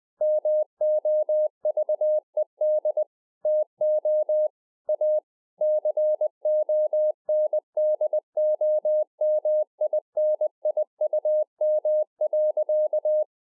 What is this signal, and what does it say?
Morse code, heard as a keyed tone, about 20 wpm, 610 Hz. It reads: MOVED TO A CONDOMINIUM.